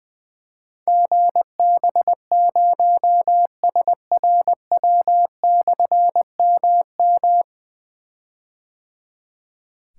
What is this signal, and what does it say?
Morse code, heard as a keyed tone, about 20 wpm, 700 Hz. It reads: GB0SRW/MM
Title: Difficult Call Signs 20wpm